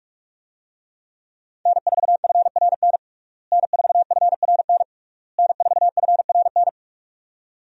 Morse code, heard as a keyed tone, about 46 wpm, 700 Hz. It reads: N4FRN N4FRN N4FRN
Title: USA Call Signs - 5 Characters - Repeated 3x - 45wpm